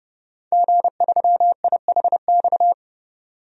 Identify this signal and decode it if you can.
Morse code, heard as a keyed tone, about 32 wpm, 700 Hz. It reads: G3IHX
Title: DXCC-100 Call Signs - 5 Characters - 30wpm